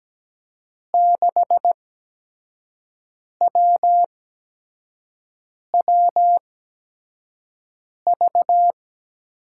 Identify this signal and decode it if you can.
Morse code, heard as a keyed tone, about 17 wpm, spaced as for 5 wpm, 700 Hz. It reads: 6WWV
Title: USA Call Signs - 8x Character Spacing - 17wpm